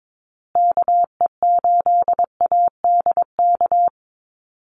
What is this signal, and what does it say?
Morse code, heard as a keyed tone, about 22 wpm, 700 Hz. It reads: KE8ADK
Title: USA Call Signs Repeated 2x 22wpm